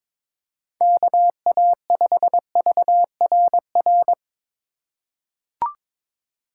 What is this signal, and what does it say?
Morse code, heard as a keyed tone, about 21 wpm, 700 Hz. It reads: KA5VRR
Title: USA Call Signs - 6 Characters - 3x Character Spacing - 22wpm